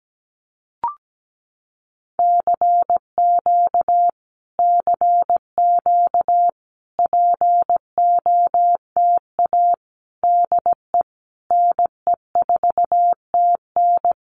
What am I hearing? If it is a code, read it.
Morse code, heard as a keyed tone, about 17 wpm, 700 Hz. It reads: CQ CQ POTA DE NE4TN
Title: Common QSO Phrases Encore 17wpm